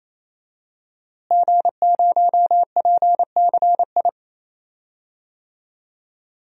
Morse code, heard as a keyed tone, about 27 wpm, 700 Hz. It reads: G0PCI
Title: DXCC-100 Call Signs - 5 Characters - Repeated 3x - 28wpm